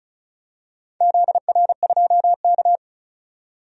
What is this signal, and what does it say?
Morse code, heard as a keyed tone, about 34 wpm, 700 Hz. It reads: ZR2K
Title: DXCC-100 Call Signs - 4 Characters - 35wpm